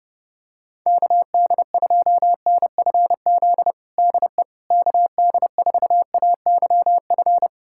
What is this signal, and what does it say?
Morse code, heard as a keyed tone, about 30 wpm, 700 Hz. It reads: KD2NFZ DE KD4AYF
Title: USA Callsign DE Callsign - 6 Characters - 30wpm